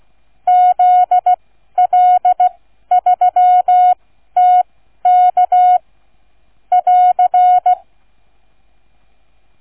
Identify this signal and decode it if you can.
Morse code, heard as a keyed tone, about 16 wpm, spaced as for 12 wpm, 730 Hz. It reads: ZL3TK <AR>